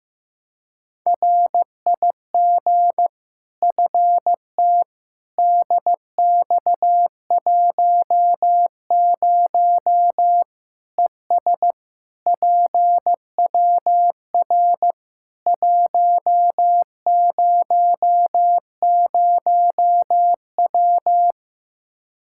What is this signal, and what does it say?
Morse code, heard as a keyed tone, about 15 wpm, 700 Hz. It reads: RIG FT DX10 ES PWR 100W
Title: K1USN SST QSO Practice 15wpm